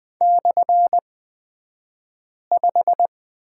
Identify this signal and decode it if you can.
Morse code, heard as a keyed tone, about 20 wpm, spaced as for 5 wpm, 700 Hz. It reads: /5
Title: 235 Lesson 38d - Letter q - Two Characters 20wpm